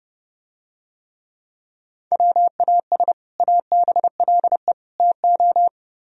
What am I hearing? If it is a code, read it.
Morse code, heard as a keyed tone, about 32 wpm, 700 Hz. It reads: WAS ABLE TO